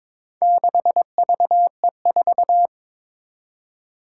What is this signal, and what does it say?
Morse code, heard as a keyed tone, about 22 wpm, 700 Hz. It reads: 6VE4
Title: Four Challenging Letter-Number Combinations 22wpm